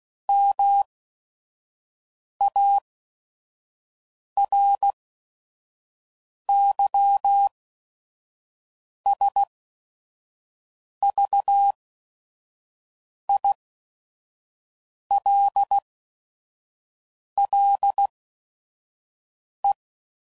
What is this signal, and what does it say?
Morse code, heard as a keyed tone, about 16 wpm, spaced as for 5 wpm, 800 Hz. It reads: MARYSVILLE
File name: CW Code Groups @5WPM / fromSM's / Cities